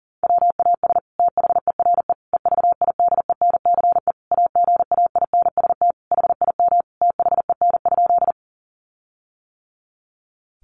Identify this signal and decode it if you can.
Morse code, heard as a keyed tone, about 40 wpm, 700 Hz. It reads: WAS THERE EVIDENCE AGAINST HIM THEN?
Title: Sentences from Top 1300 Words 40wpm